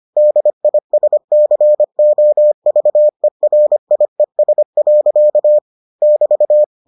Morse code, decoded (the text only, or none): DISCOVERIES. <BT>